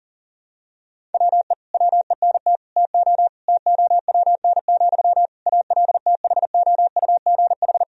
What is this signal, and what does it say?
Morse code, heard as a keyed tone, about 40 wpm, 700 Hz. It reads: WE WENT TO TOWN, ALTHOUGH
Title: Sentences from Top 1000 Words - Review Part 3 - 40wpm